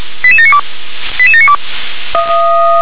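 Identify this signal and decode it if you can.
Five-tone selective call (ZVEI), heard as quick runs of five tones, tones of about 70 ms. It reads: 88792 88792
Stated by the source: Five-tone ZVEI selective call received on an SDR.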